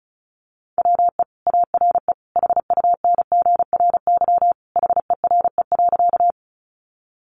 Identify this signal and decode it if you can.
Morse code, heard as a keyed tone, about 34 wpm, 700 Hz. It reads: WE ARE HUNGRY HERE.
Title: Sentences from Top 1400 Words 35wpm